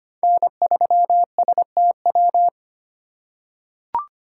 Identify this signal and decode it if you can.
Morse code, heard as a keyed tone, about 25 wpm, 700 Hz. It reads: N3STW
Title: USA Call Signs - 5 Characters - 3x Character Spacing - 25wpm